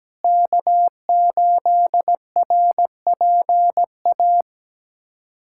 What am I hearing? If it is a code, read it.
Morse code, heard as a keyed tone, about 17 wpm, 700 Hz. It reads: K8RPA
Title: USA Callsign DE Callsign - 5 Characters - 17wpm